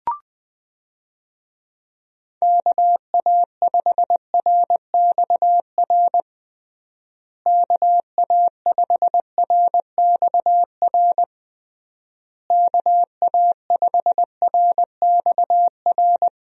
Morse code, heard as a keyed tone, about 20 wpm, 700 Hz. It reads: KA5RXR KA5RXR KA5RXR
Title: USA Call Signs - 6 Characters - Repeated 3x - 20wpm